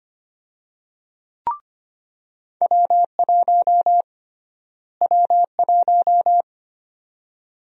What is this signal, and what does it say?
Morse code, heard as a keyed tone, about 25 wpm, 700 Hz. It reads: W1 W1
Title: USA Call Sign Prefixes - Repeated 2x - 25wpm